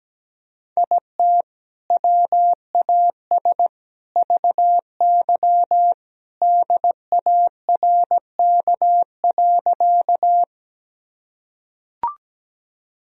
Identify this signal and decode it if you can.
Morse code, heard as a keyed tone, about 17 wpm, 700 Hz. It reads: IT WAS VY DARK.
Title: Sentences from Top 500 Words and Abbreviations 17wpm